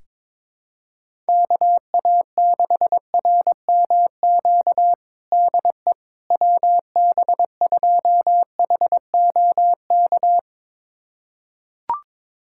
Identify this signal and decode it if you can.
Morse code, heard as a keyed tone, about 22 wpm, 700 Hz. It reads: KA6RMQ DE WB2HOK
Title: USA Callsign DE Callsign - 6 Characters - 22wpm